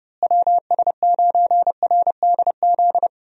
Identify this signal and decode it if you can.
Morse code, heard as a keyed tone, about 30 wpm, 700 Hz. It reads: WS9RDZ
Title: USA Call Signs - 6 Characters - Repeated 2x - 30wpm